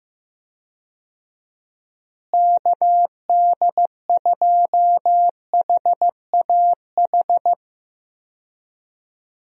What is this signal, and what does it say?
Morse code, heard as a keyed tone, about 15 wpm, 700 Hz. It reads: KD2HAH
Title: USA Call Signs - 6 Characters - Repeated 3x - 15wpm